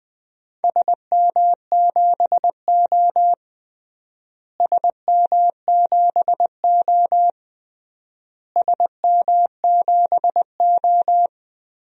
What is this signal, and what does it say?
Morse code, heard as a keyed tone, about 20 wpm, 700 Hz. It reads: SM7O SM7O SM7O
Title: DXCC-100 Call Signs - 4 Characters - Repeated 3x - 20wpm